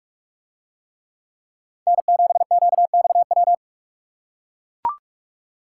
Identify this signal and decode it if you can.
Morse code, heard as a keyed tone, about 42 wpm, 700 Hz. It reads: N7QXW